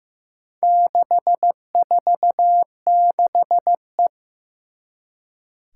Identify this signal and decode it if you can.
Morse code, heard as a keyed tone, about 15 wpm, 700 Hz. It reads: 646E
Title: Four Challenging Letter-Number Combinations 15wpm